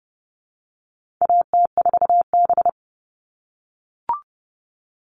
Morse code, heard as a keyed tone, about 30 wpm, 700 Hz. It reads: AT4B
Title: All Cut Numbers - 4-Digit 30wpm